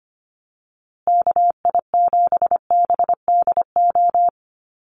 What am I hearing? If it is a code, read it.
Morse code, heard as a keyed tone, about 25 wpm, 700 Hz. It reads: KI7BDO
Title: USA Call Signs Repeated 2x 25wpm